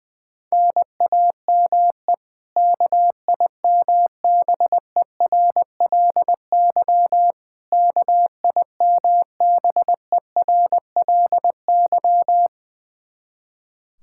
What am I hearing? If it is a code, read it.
Morse code, heard as a keyed tone, about 21 wpm, 700 Hz. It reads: NAME KIMBERLY KIMBERLY
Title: Practice QSO - Exchange 1 - Beginning - 20wpm